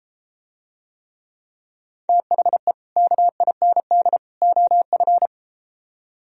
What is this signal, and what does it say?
Morse code, heard as a keyed tone, about 34 wpm, 700 Hz. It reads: THE KIND OF